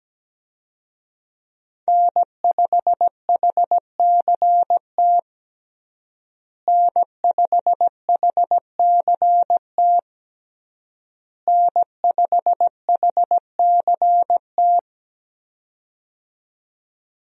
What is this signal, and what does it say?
Morse code, heard as a keyed tone, about 17 wpm, 700 Hz. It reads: N5HCT N5HCT N5HCT
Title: USA Call Signs - 5 Characters - Repeated 3x - 17wpm